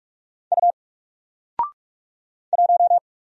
Morse code, heard as a keyed tone, about 46 wpm, 700 Hz. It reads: U 1
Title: Single Letter-Number Alt 45wpm